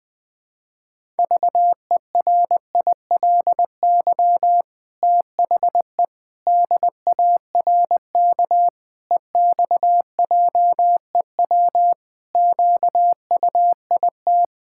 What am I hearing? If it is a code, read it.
Morse code, heard as a keyed tone, about 20 wpm, 700 Hz. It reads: VERILY THE DARK EXJEW QUIT
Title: Pangram Challenge v2 - 20wpm